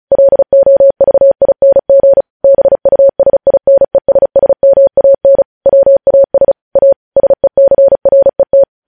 Morse code, heard as a keyed tone, about 36 wpm, 550 Hz. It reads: LOVING BUSINESSMAN WAS A SECRET
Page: news headlines in morse code at 35 wpm